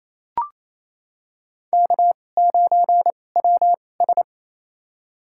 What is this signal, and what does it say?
Morse code, heard as a keyed tone, about 28 wpm, spaced as for 20 wpm, 700 Hz. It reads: K9WS
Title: USA Call Signs - 4 Characters - 2x Character Spacing - 28wpm